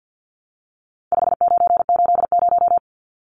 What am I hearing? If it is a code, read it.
Morse code, heard as a keyed tone, about 50 wpm, 700 Hz. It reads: 5980
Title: Numbers - 4-Digit 50wpm